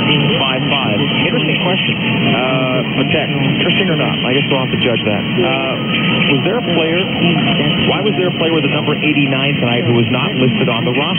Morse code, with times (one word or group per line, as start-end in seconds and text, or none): none